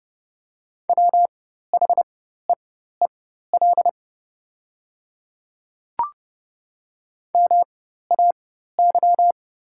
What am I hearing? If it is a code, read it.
Morse code, heard as a keyed tone, about 30 wpm, spaced as for 14 wpm, 700 Hz. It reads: WHEEL MAY
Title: 4x Character Spacing - Top 500 Words - 30wpm